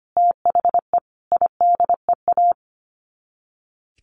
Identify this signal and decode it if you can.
Morse code, heard as a keyed tone, about 25 wpm, 700 Hz. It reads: THE IDEA
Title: Sets of 5 Words Encore2 25wpm